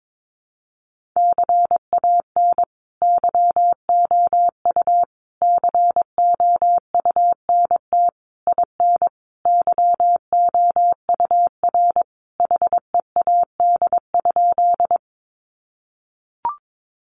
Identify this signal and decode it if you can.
Morse code, heard as a keyed tone, about 22 wpm, 700 Hz. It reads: CAN YOU COUNT IN YOUR HEAD?
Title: Sentences from Top 1500 Words 22wpm